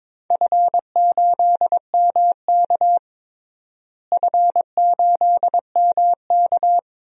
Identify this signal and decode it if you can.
Morse code, heard as a keyed tone, about 22 wpm, 700 Hz. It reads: F8MK F8MK
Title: DXCC-100 Call Signs - 4 Characters - Repeated 2x - 22wpm